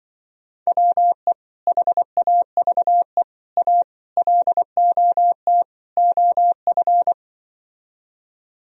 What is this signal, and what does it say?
Morse code, heard as a keyed tone, about 24 wpm, 700 Hz. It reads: WE HAVE A LOT OF
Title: Speed-Racing - Sets of 5 Words 20wpm